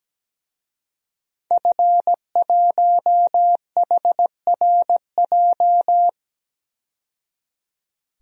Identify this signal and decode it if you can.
Morse code, heard as a keyed tone, about 17 wpm, 700 Hz. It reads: F1HRJ
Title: DXCC-100 Call Signs - 5 Characters - 17wpm